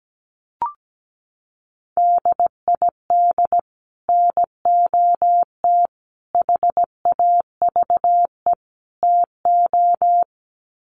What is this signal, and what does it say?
Morse code, heard as a keyed tone, about 17 wpm, 700 Hz. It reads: DID NOT HAVE TO
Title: Sets of 4 Words 17wpm